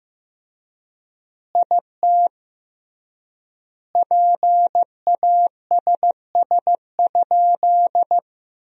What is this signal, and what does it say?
Morse code, heard as a keyed tone, about 15 wpm, 700 Hz. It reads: IT PASS?
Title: 3x Word-Spacing - Sentences from Top 400 Words - 15wpm